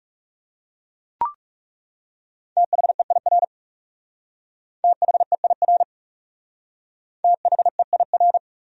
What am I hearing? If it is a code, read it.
Morse code, heard as a keyed tone, about 38 wpm, 700 Hz. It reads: THEIR THEIR THEIR